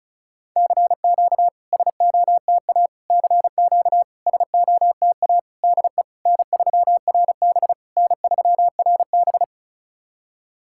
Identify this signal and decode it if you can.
Morse code, heard as a keyed tone, about 36 wpm, 700 Hz. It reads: CQ SOTA CQ SOTA DE N3RB N3RB
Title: SOTA QSO Practice 35wpm